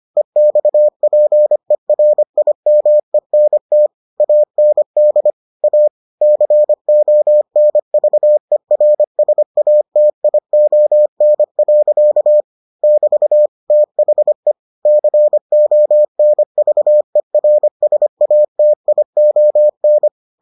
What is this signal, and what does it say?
Morse code, heard as a keyed tone, about 25 wpm, 600 Hz. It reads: EXPERIMENT AND A CONVERSATION. <BT> THE CONVERSATION